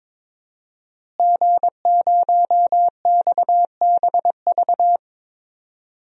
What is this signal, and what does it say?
Morse code, heard as a keyed tone, about 22 wpm, 700 Hz. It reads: G0XBV
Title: DXCC-100 Call Signs - 5 Characters - Repeated 3x - 22wpm